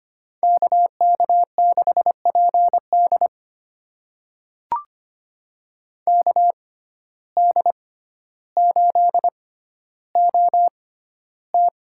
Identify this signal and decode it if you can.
Morse code, heard as a keyed tone, about 25 wpm, spaced as for 13 wpm, 700 Hz. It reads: KK6PD KD8OT
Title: USA Call Signs - 5 Characters - 6x Character Spacing - 25wpm